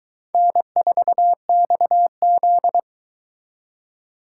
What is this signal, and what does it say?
Morse code, heard as a keyed tone, about 22 wpm, 700 Hz. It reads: N4XZ